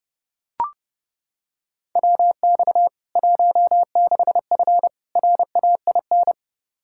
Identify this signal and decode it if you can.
Morse code, heard as a keyed tone, about 30 wpm, 700 Hz. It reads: WX 16F RAIN
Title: K1USN SST QSO Practice 30wpm